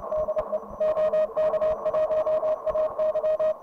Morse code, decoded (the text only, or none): DE OK1AY